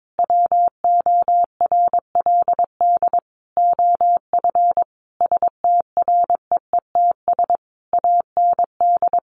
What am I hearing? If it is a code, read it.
Morse code, heard as a keyed tone, about 22 wpm, 700 Hz. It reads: WORLD OF STREETS AND
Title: A Sound Of Thunder 22wpm